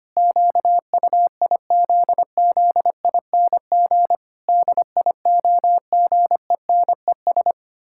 Morse code, heard as a keyed tone, about 25 wpm, 700 Hz. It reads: QUIZZING DIOGENES
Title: Pangram Challenge v2 - 25wpm